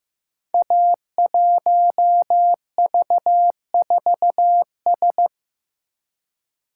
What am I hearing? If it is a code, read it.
Morse code, heard as a keyed tone, about 15 wpm, 700 Hz. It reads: A1V4S
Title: Code Groups - 5 Letter:Numbers - 15wpm